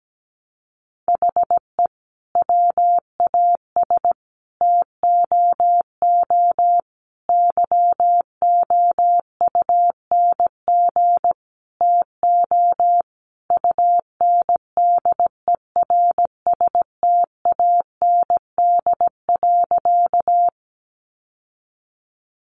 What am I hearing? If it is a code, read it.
Morse code, heard as a keyed tone, about 17 wpm, 700 Hz. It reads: HE WAS TOO YOUNG TO UNDERSTAND.
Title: Sentences from Top 600 Words 17wpm